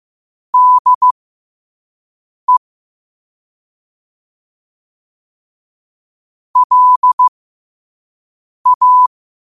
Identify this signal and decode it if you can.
Morse code, heard as a keyed tone, about 15 wpm, spaced as for 5 wpm, 1000 Hz. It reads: DE LA
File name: qotdfiles / 5wpm